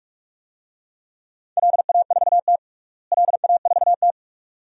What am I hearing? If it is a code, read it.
Morse code, heard as a keyed tone, about 47 wpm, 700 Hz. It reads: LA4T LA4T
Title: DXCC-100 Call Signs - 4 Characters - Repeated 2x - 45wpm